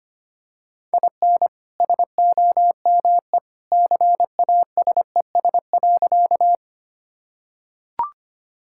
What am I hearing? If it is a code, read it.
Morse code, heard as a keyed tone, about 25 wpm, 690 Hz. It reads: IN SOME CASES.
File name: practice / 25wpm / Sentences with 2-Word Phrases - 25wpm